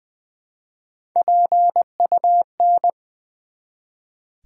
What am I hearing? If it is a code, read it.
Morse code, heard as a keyed tone, about 20 wpm, 700 Hz. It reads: PUN
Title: Build A Word - 1 Letter At A Time - 5-Letter Words - 20wpm